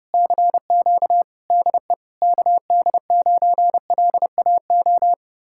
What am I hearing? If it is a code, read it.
Morse code, heard as a keyed tone, about 33 wpm, 700 Hz. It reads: CQ DE KD9LAO